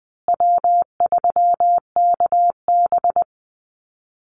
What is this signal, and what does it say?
Morse code, heard as a keyed tone, about 19 wpm, 700 Hz. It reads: W3KB
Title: Easy Call Signs Repeated 3x - 20wpm